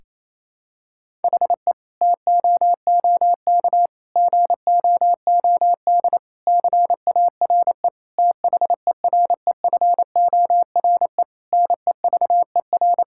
Morse code, heard as a keyed tone, about 28 wpm, 700 Hz. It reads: HE TOOK GOOD CARE THEREFORE NEVER
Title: Sentences from Top 1000 Words - Review Part 5 - 28wpm